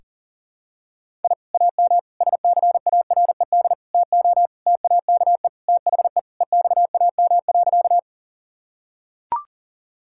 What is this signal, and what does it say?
Morse code, heard as a keyed tone, about 40 wpm, 700 Hz. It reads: I AM SCARED TO TAKE THE EXAM.
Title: Sentences from Top 1000 Words - Review Part 5 - 40wpm